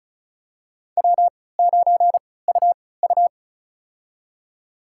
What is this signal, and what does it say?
Morse code, heard as a keyed tone, about 34 wpm, spaced as for 20 wpm, 700 Hz. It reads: W9UU
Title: USA Call Signs - 4 Characters - 3x Character Spacing - 35wpm